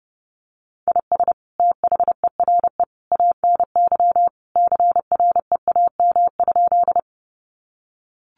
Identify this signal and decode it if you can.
Morse code, heard as a keyed tone, about 29 wpm, 700 Hz. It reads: IS THERE ANY CREAM?
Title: Sentences from Top 1500 Words 30wpm